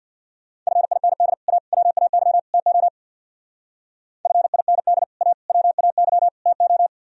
Morse code, heard as a keyed tone, about 64 wpm, 700 Hz. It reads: FIND A WAY TO FIND A WAY TO